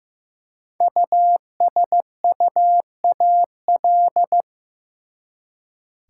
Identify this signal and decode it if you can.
Morse code, heard as a keyed tone, about 15 wpm, 700 Hz. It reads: USUAL
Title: Five Letter Words - 15wpm